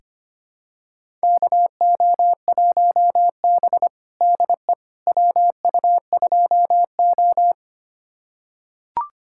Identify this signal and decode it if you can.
Morse code, heard as a keyed tone, about 25 wpm, 700 Hz. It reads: KO1B DE WU2O
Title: USA Callsign DE Callsign - 4 Characters - 25wpm